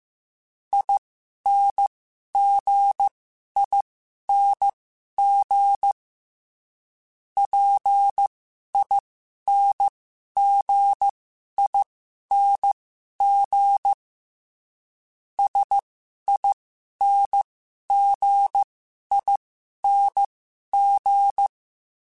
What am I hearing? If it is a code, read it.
Morse code, heard as a keyed tone, about 15 wpm, spaced as for 11 wpm, 800 Hz. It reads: INGING PINGING SINGING
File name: Special INGs / Special INGs-mp3s / Dble ings @10